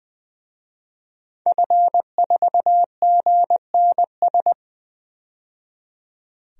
Morse code, heard as a keyed tone, about 21 wpm, 700 Hz. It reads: F4GNS
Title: DXCC-100 Call Signs - 5 Characters - 20wpm